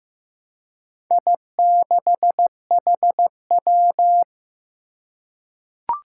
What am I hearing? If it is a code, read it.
Morse code, heard as a keyed tone, about 15 wpm, 700 Hz. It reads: I6HW
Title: Four Challenging Letter-Number Combinations 15wpm